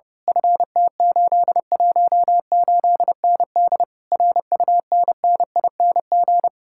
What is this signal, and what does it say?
Morse code, heard as a keyed tone, about 28 wpm, 700 Hz. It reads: FT818ND RUNNING
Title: Speed-Racing Common QSO Phrases Encore - 30wpm